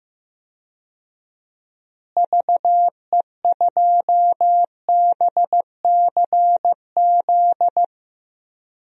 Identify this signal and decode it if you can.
Morse code, heard as a keyed tone, about 15 wpm, 700 Hz. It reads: VE2BCZ
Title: DXCC-100 Call Signs - 6 Characters - Repeated 2x - 15wpm